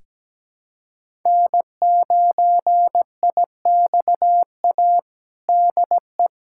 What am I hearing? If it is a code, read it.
Morse code, heard as a keyed tone, about 17 wpm, 700 Hz. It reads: N9IXA DE
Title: USA Callsign DE Callsign - 5 Characters - 17wpm